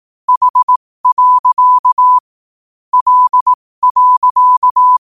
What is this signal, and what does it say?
Morse code, heard as a keyed tone, about 18 wpm, spaced as for 15 wpm, 1000 Hz. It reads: H. L.
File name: qotdfiles / 15wpm